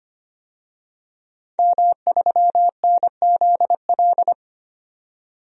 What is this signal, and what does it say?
Morse code, heard as a keyed tone, about 25 wpm, 700 Hz. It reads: M3NZL
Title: DXCC-100 Call Signs - 5 Characters - 25wpm